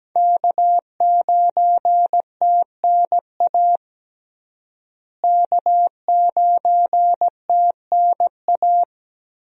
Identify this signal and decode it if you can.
Morse code, heard as a keyed tone, about 17 wpm, 700 Hz. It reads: K9TNA K9TNA
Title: USA Call Signs - 5 Characters - Repeated 3x - 17wpm